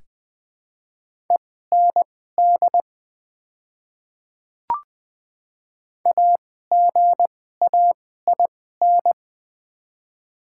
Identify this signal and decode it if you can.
Morse code, heard as a keyed tone, about 20 wpm, spaced as for 14 wpm, 700 Hz. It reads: END AGAIN
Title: 2x Character Spacing - Top 200 Words - 20wpm